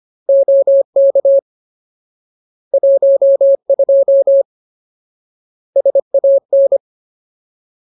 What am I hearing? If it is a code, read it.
Morse code, heard as a keyed tone, about 25 wpm, 550 Hz. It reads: OK 12 SAN
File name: Ultimate Word List / 25WPM Ultimate